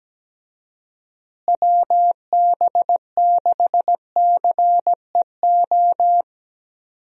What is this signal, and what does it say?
Morse code, heard as a keyed tone, about 17 wpm, 700 Hz. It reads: WB6CEO
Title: USA Call Signs - 6 Characters - Repeated 3x - 17wpm